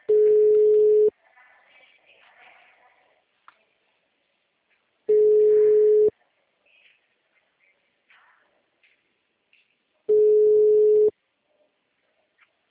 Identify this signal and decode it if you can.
Ringback tone (Europe, 425 Hz).
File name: out- -2013- - - .54867